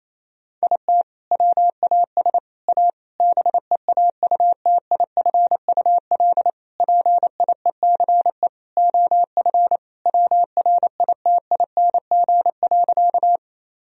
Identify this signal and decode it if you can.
Morse code, heard as a keyed tone, about 28 wpm, 700 Hz. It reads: IT WAS A BEAUTIFUL PIECE OF WRITING.
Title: Sentences from Top 1000 Words - Review Part 2 - 28wpm